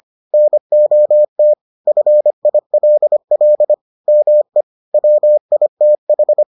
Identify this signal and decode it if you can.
Morse code, heard as a keyed tone, about 25 wpm, 600 Hz. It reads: NOT FILL ME WITH